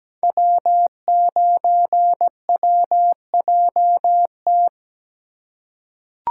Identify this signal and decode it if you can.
Morse code, heard as a keyed tone, about 17 wpm, 700 Hz. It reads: W9WJT
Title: Code Groups - 5 Letter:Numbers - 17wpm